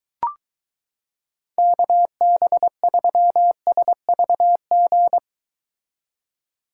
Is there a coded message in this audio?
Yes: Morse code from 1.6 to 5.2 s, heard as a keyed tone, about 22 wpm, 700 Hz. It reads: KB3SVG